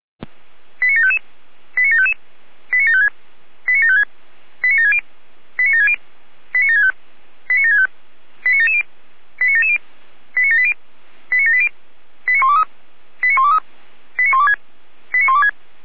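Five-tone selective call (ZVEI), heard as quick runs of five tones, tones of about 70 ms. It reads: 89755 89755 89756 89756 89760 89760 89765 89765 89770 89770 89780 89780 89123 89123 89126 89126
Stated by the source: ZVEI five-tone call received with an SDR.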